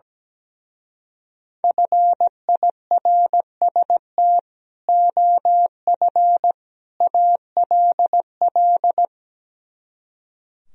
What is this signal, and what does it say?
Morse code, heard as a keyed tone, about 17 wpm, 700 Hz. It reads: FIRST OF ALL
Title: Speed-Racing - Sets of 3 Words 17wpm